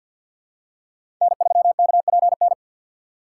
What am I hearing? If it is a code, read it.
Morse code, heard as a keyed tone, about 49 wpm, 700 Hz. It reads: N3KPN
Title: USA Call Signs - 5 Characters - 6x Character Spacing - 50wpm